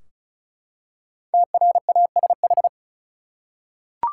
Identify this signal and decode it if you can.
Morse code, heard as a keyed tone, about 36 wpm, 700 Hz. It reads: TRASH
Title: Five Letter Words - 35wpm